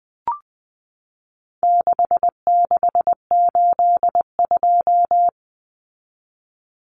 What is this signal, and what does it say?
Morse code, heard as a keyed tone, about 20 wpm, 700 Hz. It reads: 6682
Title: Numbers - 4-Digit 20wpm